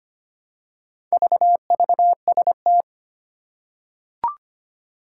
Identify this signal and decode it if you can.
Morse code, heard as a keyed tone, about 25 wpm, 700 Hz. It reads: VVST
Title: Four Challenging Letter-Number Combinations 25wpm